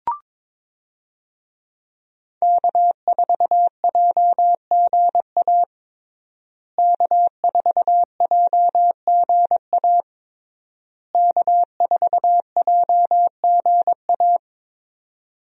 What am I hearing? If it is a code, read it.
Morse code, heard as a keyed tone, about 22 wpm, 700 Hz. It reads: K4JGA K4JGA K4JGA
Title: USA Call Signs - 5 Characters - Repeated 3x - 22wpm